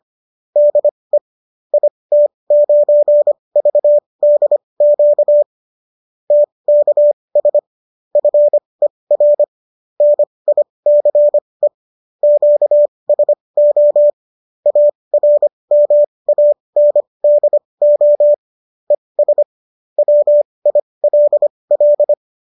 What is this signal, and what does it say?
Morse code, heard as a keyed tone, about 25 wpm, spaced as for 20 wpm, 590 Hz. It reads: DE IT9VDQ TKS FER NICE QSO ARMANDO ES WILL